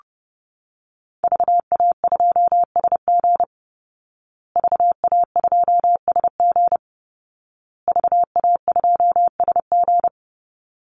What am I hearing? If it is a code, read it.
Morse code, heard as a keyed tone, about 30 wpm, 700 Hz. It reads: VA2SG VA2SG VA2SG
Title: Call Signs Repeated 2x - 30wpm